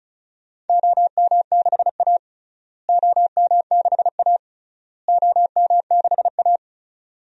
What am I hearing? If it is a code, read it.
Morse code, heard as a keyed tone, about 34 wpm, 700 Hz. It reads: OM6A OM6A OM6A
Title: DXCC-100 Call Signs - 4 Characters - Repeated 3x - 35wpm